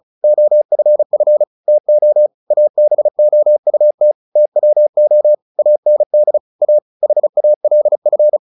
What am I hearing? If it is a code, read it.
Morse code, heard as a keyed tone, about 34 wpm, 600 Hz. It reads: OFF TO ABOUT TWO AND A HALF
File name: the old man and the sea 2 35wpm